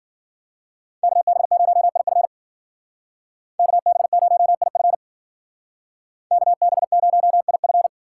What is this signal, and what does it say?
Morse code, heard as a keyed tone, about 50 wpm, 700 Hz. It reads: KB0IF KB0IF KB0IF